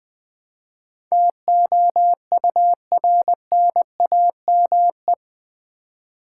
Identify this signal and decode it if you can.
Morse code, heard as a keyed tone, about 20 wpm, 700 Hz. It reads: TOURNAME
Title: Build A Word - 4 Letters At A Time - 10-Letter Words - 20wpm